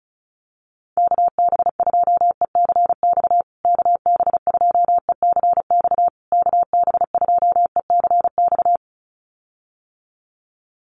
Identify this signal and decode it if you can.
Morse code, heard as a keyed tone, about 34 wpm, 700 Hz. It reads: KB2ECX KB2ECX KB2ECX
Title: USA Call Signs Repeated 3x 35wpm